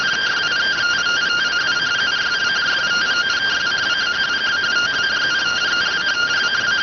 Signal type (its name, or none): two-tone teleprinter signal